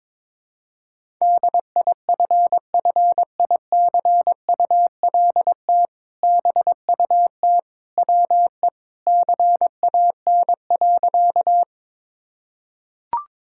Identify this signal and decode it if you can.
Morse code, heard as a keyed tone, about 21 wpm, 700 Hz. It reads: DIFFICULT BUT WE CAN.
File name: practice / 22wpm / Sentences with 2-Word Phrases - 22wpm